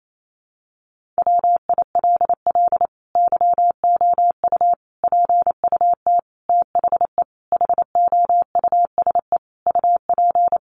Morse code, heard as a keyed tone, about 28 wpm, 700 Hz. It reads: WILL YOU PUT THE HOUSE UP
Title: Sentences from Top 1400 Words 28wpm